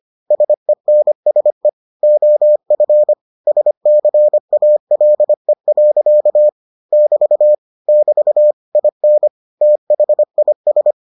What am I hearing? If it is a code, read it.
Morse code, heard as a keyed tone, about 25 wpm, 600 Hz. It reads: SENSE OF SCALE. <BT> <BT> IN THIS